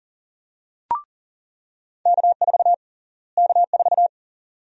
Morse code, heard as a keyed tone, about 40 wpm, 700 Hz. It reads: K4 K4
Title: USA Call Sign Prefixes - Repeated 2x - 40wpm